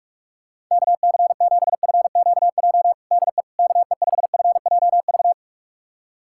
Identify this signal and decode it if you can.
Morse code, heard as a keyed tone, about 42 wpm, 700 Hz. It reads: KC7FQJ DE KE5FJV
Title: USA Callsign DE Callsign - 6 Characters - 45wpm